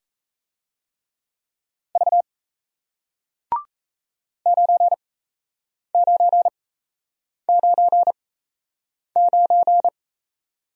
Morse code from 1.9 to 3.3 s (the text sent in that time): V